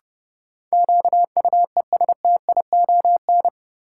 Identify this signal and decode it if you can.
Morse code, heard as a keyed tone, about 30 wpm, 700 Hz. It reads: QUESTION
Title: Sentences from Top 400 Words - New Word Drill - 30wpm